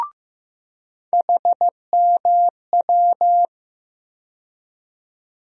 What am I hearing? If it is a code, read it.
Morse code, heard as a keyed tone, about 15 wpm, 700 Hz. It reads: HMW
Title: USA Call Sign Postfixes - 3 Letters 15wpm